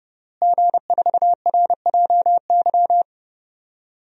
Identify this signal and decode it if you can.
Morse code, heard as a keyed tone, about 30 wpm, 700 Hz. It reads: G4RJY
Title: DXCC-100 Call Signs - 5 Characters - Repeated 3x - 30wpm